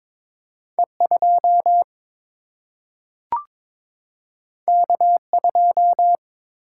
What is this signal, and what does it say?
Morse code, heard as a keyed tone, about 22 wpm, 700 Hz. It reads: E 2 K 2
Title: Single-Letter Single-Number 22wpm